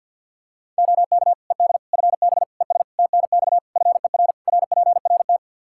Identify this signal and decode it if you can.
Morse code, heard as a keyed tone, about 48 wpm, 700 Hz. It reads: OK ED FB ES TNX FER RPRT
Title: K1USN SST QSO Practice 50wpm